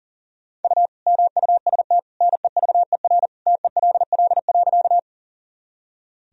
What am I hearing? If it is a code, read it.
Morse code, heard as a keyed tone, about 40 wpm, 700 Hz. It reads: U MUST NEVER TELL.
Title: Sentences from Top 300 Words and Abbreviations 40wpm